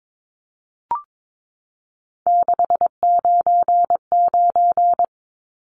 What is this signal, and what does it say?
Morse code, heard as a keyed tone, about 22 wpm, 700 Hz. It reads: 699
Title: Numbers - 3-Digit 22wpm